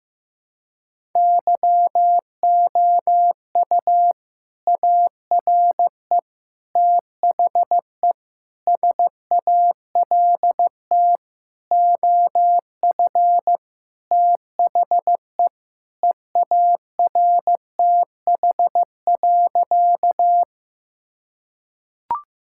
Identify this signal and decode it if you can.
Morse code, heard as a keyed tone, about 16 wpm, 700 Hz. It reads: YOU ARE THE SALT OF THE EARTH.
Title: Sentences from Top 1000 Words - Review Part 5 - 15wpm